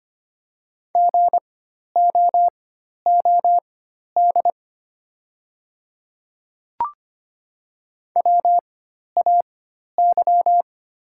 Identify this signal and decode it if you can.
Morse code, heard as a keyed tone, about 25 wpm, spaced as for 12 wpm, 700 Hz. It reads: GOOD WAY
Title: 4x Character Spacing - Top 200 Words - 25wpm